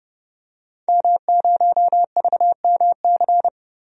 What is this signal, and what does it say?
Morse code, heard as a keyed tone, about 32 wpm, 700 Hz. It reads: M0VMC
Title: DXCC-100 Call Signs - 5 Characters - Repeated 2x - 30wpm